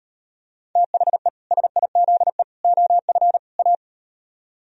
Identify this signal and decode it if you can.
Morse code, heard as a keyed tone, about 38 wpm, 700 Hz. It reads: THE SIZE OF A